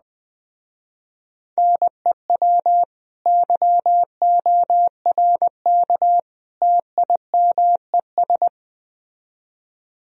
Morse code, heard as a keyed tone, about 21 wpm, 700 Hz. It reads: NEW YORK TIMES